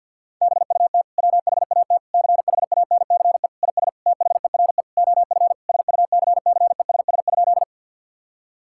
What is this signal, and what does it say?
Morse code, heard as a keyed tone, about 50 wpm, 700 Hz. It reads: BUT WHAT CHANCE IS THERE OF SUCCESS?
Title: Sentences from Top 1000 Words - Review Part 6 - 50wpm